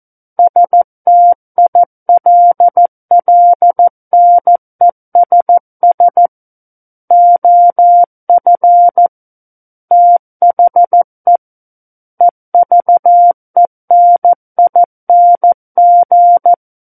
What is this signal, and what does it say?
Morse code, heard as a keyed tone, about 14 wpm, 700 Hz. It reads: STILLNESS OF THE EVENING
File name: Book One / 05-Chapter 5 - 14 WPM